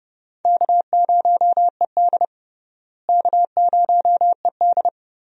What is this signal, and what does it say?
Morse code, heard as a keyed tone, about 30 wpm, 700 Hz. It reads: K0ED K0ED
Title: USA Call Signs - 4 Characters - Repeated 3x - 30wpm